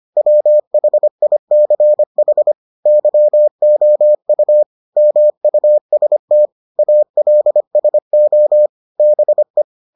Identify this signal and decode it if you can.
Morse code, heard as a keyed tone, about 25 wpm, 600 Hz. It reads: WHICH YOU MUST ALSO BE